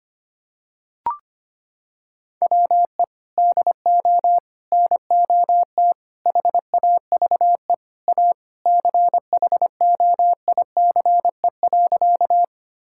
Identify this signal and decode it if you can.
Morse code, heard as a keyed tone, about 25 wpm, 700 Hz. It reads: WE DO NOT HAVE A CHOICE.
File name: practice / 25wpm / Sentences with 3-Word Phrases - 25wpm